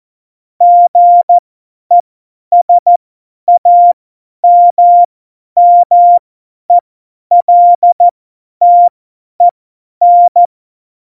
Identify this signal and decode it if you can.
Morse code, heard as a keyed tone, about 14 wpm, spaced as for 10 wpm, 700 Hz. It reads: GESAMMELTEN
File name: Morsekurs / Witze / Witz 21. Mai 2019, 50 F70